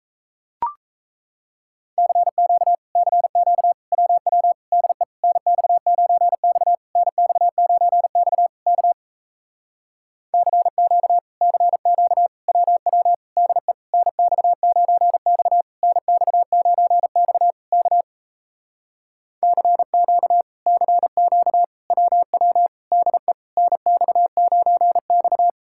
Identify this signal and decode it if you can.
Morse code, heard as a keyed tone, about 38 wpm, 700 Hz. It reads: CQ CQ WW DE NX9X NX9X K CQ CQ WW DE NX9X NX9X K CQ CQ WW DE NX9X